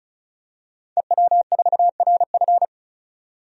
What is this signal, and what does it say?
Morse code, heard as a keyed tone, about 36 wpm, 700 Hz. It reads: EW4RF
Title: DXCC-100 Call Signs - 5 Characters - 35wpm